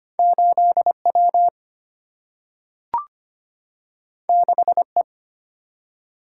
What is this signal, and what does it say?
Morse code, heard as a keyed tone, about 25 wpm, 700 Hz. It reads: 8 W 6 E
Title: Single-Letter Single-Number 25wpm